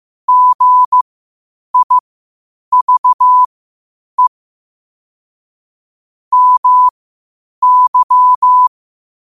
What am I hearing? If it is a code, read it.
Morse code, heard as a keyed tone, about 15 wpm, spaced as for 8 wpm, 1000 Hz. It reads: GIVE MY
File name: qotdfiles / 8wpm